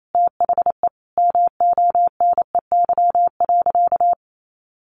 Morse code, heard as a keyed tone, about 28 wpm, 700 Hz. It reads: THE MONEY.
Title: Sentences from Top 600 Words 28wpm